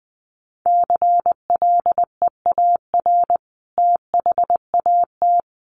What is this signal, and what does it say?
Morse code, heard as a keyed tone, about 20 wpm, 700 Hz. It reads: CLEAR THAT
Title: Sentences from Top 600 Words 20wpm